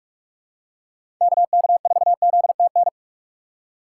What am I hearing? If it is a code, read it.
Morse code, heard as a keyed tone, about 45 wpm, 700 Hz. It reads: KK4ZTN
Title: USA Call Signs - 6 Characters - 4x Character Spacing - 45wpm